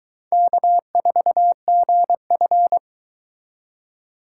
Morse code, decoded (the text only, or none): K4GF